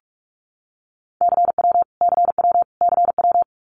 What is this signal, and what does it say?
Morse code, heard as a keyed tone, about 44 wpm, 700 Hz. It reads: CW CW CW
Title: Abbreviations Repeated 3x - 45wpm